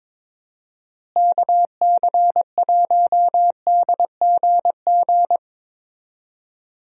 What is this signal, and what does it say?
Morse code, heard as a keyed tone, about 22 wpm, 700 Hz. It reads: KC1DGG
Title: USA Call Signs - 6 Characters - Repeated 2x - 22wpm